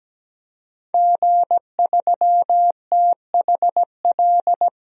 Morse code, heard as a keyed tone, about 17 wpm, 700 Hz. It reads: G3THL
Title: DXCC-100 Call Signs - 5 Characters - 17wpm